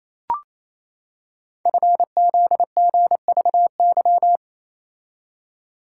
Morse code, heard as a keyed tone, about 27 wpm, 700 Hz. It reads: FZGVY
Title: Code Groups - 5 Letters - 28wpm